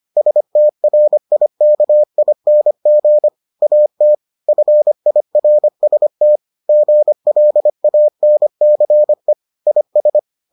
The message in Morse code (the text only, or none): STRIKING AT FIRST GLANCE IS